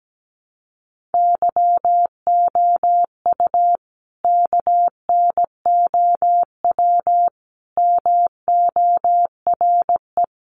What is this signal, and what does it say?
Morse code, heard as a keyed tone, about 17 wpm, 700 Hz. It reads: YOU KNOW MORE
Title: Sentences from Top 100 Words 17wpm